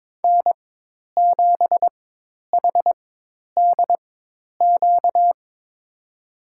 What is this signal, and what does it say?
Morse code, heard as a keyed tone, about 22 wpm, spaced as for 10 wpm, 700 Hz. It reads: N7HDQ
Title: USA Call Signs - 5 Characters - 4x Character Spacing - 22wpm